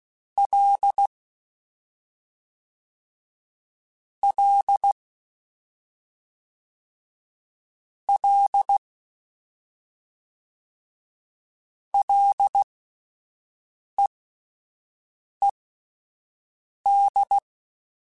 Morse code, heard as a keyed tone, about 16 wpm, spaced as for 5 wpm, 800 Hz. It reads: L L L LEED
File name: Ls / Single L m3s / Single L Prefixe 3x2 Twice @5